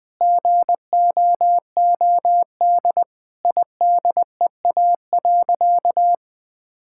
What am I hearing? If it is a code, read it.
Morse code, heard as a keyed tone, about 19 wpm, 700 Hz. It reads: GOOD IDEA.
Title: Sentences with 5-Word Phrases - Spoken Beginning - 20wpm